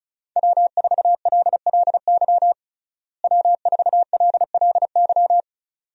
Morse code, heard as a keyed tone, about 34 wpm, 700 Hz. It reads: W4LLY W4LLY
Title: USA Call Signs - 5 Characters - Repeated 2x - 35wpm